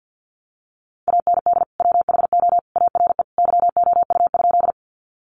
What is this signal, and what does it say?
Morse code, heard as a keyed tone, about 50 wpm, 700 Hz. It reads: AND WHO ARE YOU?
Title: Sentences from Top 100 Words 50wpm